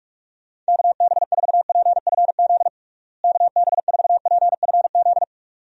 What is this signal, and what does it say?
Morse code, heard as a keyed tone, about 46 wpm, 700 Hz. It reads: KB4PFZ KB4PFZ
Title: USA Call Signs - 6 Characters - Repeated 2x - 45wpm